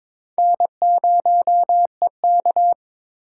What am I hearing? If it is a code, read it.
Morse code, heard as a keyed tone, about 22 wpm, 700 Hz. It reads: N0EK
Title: USA Call Signs - 4 Characters - 8x Character Spacing - 22wpm